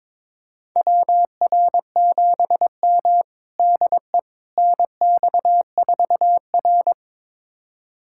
Morse code, heard as a keyed tone, about 22 wpm, 700 Hz. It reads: WR7M DE NX4R
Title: USA Callsign DE Callsign - 4 Characters - 22wpm